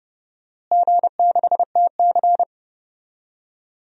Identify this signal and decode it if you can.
Morse code, heard as a keyed tone, about 30 wpm, 700 Hz. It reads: G6TC
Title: DXCC-100 Call Signs - 4 Characters - Repeated 2x - 30wpm